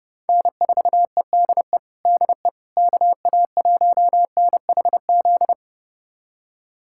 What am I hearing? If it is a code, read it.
Morse code, heard as a keyed tone, about 30 wpm, 700 Hz. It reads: N4EDE DE KA1NHZ
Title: USA Callsign DE Callsign 30wpm